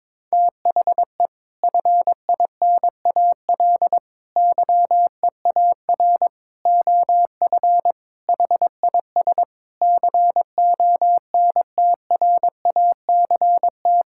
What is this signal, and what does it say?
Morse code, heard as a keyed tone, about 21 wpm, 700 Hz. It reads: THE FINAL YEAR OF HIS CONTRACT
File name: practice / 22wpm / Sets of 6 Words - 22wpm